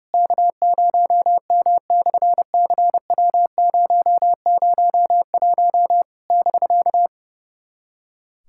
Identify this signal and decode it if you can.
Morse code, heard as a keyed tone, about 30 wpm, 700 Hz. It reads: K0M/CW001 <BK>
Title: SOTA QSO Practice 30wpm